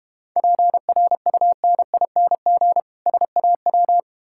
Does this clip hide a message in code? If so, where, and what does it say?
Yes: Morse code from 0.4 to 4.0 s, heard as a keyed tone, about 32 wpm, 700 Hz. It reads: PRUNING SAW